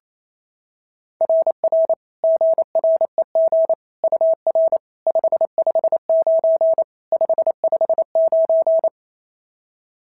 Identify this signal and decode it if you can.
Morse code, heard as a keyed tone, about 25 wpm, 650 Hz. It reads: RR GREG UR 559 559
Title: SOTA QSO Practice 28wpm